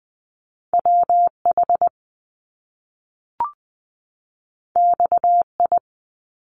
Morse code, heard as a keyed tone, about 20 wpm, 700 Hz. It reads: WH XI
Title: Top Two Letter Combinations in Words 20wpm